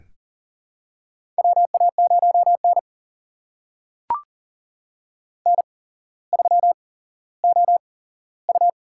Morse code, heard as a keyed tone, about 40 wpm, spaced as for 17 wpm, 700 Hz. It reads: WA0N N3OU
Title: USA Call Signs - 4 Characters - 8x Character Spacing - 40wpm